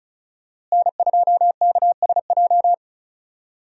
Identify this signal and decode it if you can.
Morse code, heard as a keyed tone, about 36 wpm, 700 Hz. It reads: N2KSJ
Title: USA Call Signs - 5 Characters - Repeated 3x - 35wpm